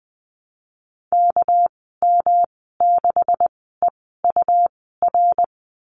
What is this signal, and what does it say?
Morse code, heard as a keyed tone, about 20 wpm, spaced as for 15 wpm, 700 Hz. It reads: KM6EUR
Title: 200 Lesson 32g - Letter k - Call Signs 20wpm